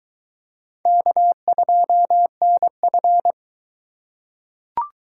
Morse code, heard as a keyed tone, about 21 wpm, 700 Hz. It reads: K2NF